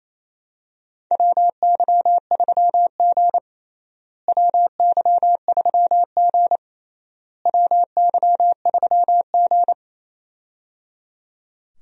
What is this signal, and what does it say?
Morse code, heard as a keyed tone, about 28 wpm, 700 Hz. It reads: WY3G WY3G WY3G
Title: USA Call Signs - 4 Characters - Repeated 3x - 28wpm